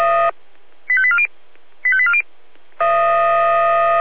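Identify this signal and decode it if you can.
Five-tone selective call (ZVEI), heard as quick runs of five tones, tones of about 70 ms. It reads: 85840 85840
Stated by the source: ZVEI five-tone call 85870 received on an SDR.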